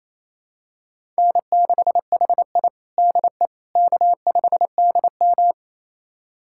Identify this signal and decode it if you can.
Morse code, heard as a keyed tone, about 28 wpm, 700 Hz. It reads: N6HI DE K5DM
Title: USA Callsign DE Callsign - 4 Characters - 28wpm